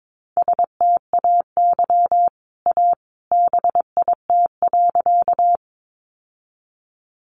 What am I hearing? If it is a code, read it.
Morse code, heard as a keyed tone, about 22 wpm, 700 Hz. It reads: STAY A BIT.
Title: Sentences from Top 900 Words 22wpm